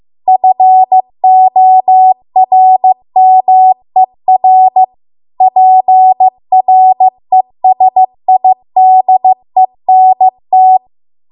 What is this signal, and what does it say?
Morse code, heard as a keyed tone, about 15 wpm, 750 Hz. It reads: FORMER PRESIDENT